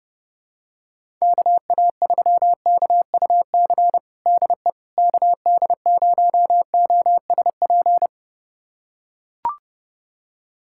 Morse code, heard as a keyed tone, about 30 wpm, 700 Hz. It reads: KA3KUC DE KD0OSP
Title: USA Callsign DE Callsign 30wpm